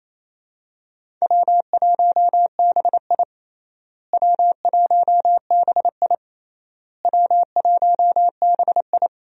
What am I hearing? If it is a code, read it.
Morse code, heard as a keyed tone, about 28 wpm, 700 Hz. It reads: W1BI W1BI W1BI
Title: USA Call Signs - 4 Characters - Repeated 3x - 28wpm